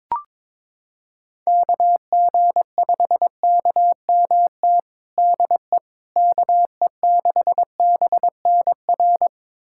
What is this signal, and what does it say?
Morse code, heard as a keyed tone, about 22 wpm, 700 Hz. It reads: KG5KMT DE KE6BNR
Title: USA Callsign DE Callsign - 6 Characters - 22wpm